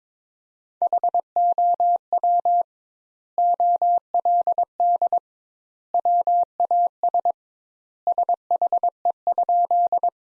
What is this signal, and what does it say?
Morse code, heard as a keyed tone, about 22 wpm, 700 Hz. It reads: HOW OLD WAS SHE?
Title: Sentences Top 200 Words - 22wpm